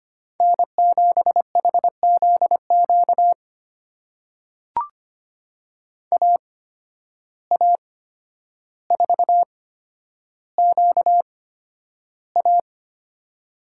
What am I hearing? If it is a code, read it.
Morse code, heard as a keyed tone, about 25 wpm, spaced as for 11 wpm, 700 Hz. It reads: N7HZQ AA4QA
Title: USA Call Signs - 5 Characters - 8x Character Spacing - 25wpm